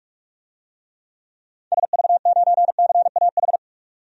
Morse code, heard as a keyed tone, about 46 wpm, 700 Hz. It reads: SV9CAH
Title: DXCC-100 Call Signs - 6 Characters - 45wpm